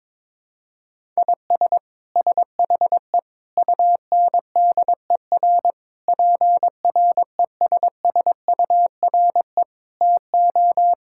Morse code, heard as a keyed tone, about 21 wpm, 700 Hz. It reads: IS SHE UNDER PRESSURE TO
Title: Sentences from Top 1000 Words - Review Part 6 - 22wpm